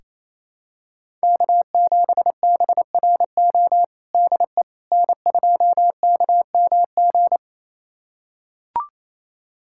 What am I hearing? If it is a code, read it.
Morse code, heard as a keyed tone, about 28 wpm, 700 Hz. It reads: K7BRO DE N2KMG
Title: USA Callsign DE Callsign - 5 Characters - 28wpm